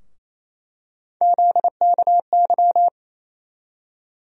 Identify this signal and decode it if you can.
Morse code, heard as a keyed tone, about 28 wpm, 700 Hz. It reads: ZKY
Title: USA Call Sign Postfixes - 3 Letters 28wpm